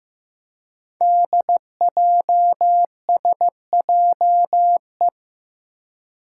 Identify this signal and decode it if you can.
Morse code, heard as a keyed tone, about 15 wpm, 700 Hz. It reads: DJSJE
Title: Five Challenging Letter-Number Combinations 15wpm